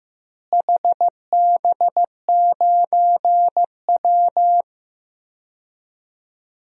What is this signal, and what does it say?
Morse code, heard as a keyed tone, about 15 wpm, 700 Hz. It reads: HB9W
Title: DXCC-100 Call Signs - 4 Characters - Repeated 2x - 15wpm